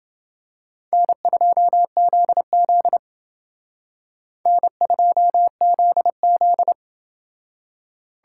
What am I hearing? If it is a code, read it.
Morse code, heard as a keyed tone, about 29 wpm, 700 Hz. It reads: N2ZZ N2ZZ